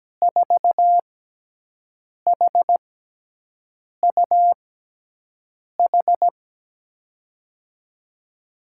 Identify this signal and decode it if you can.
Morse code, heard as a keyed tone, about 17 wpm, spaced as for 6 wpm, 700 Hz. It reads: 4HUH
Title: USA Call Signs - 6x Character Spacing - 17wpm